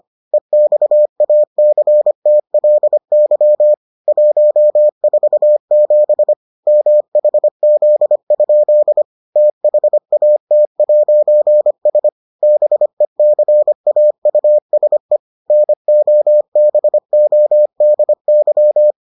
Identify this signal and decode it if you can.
Morse code, heard as a keyed tone, about 25 wpm, 600 Hz. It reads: EXACTLY 147 MHZ? THAT'S BECAUSE NOBODY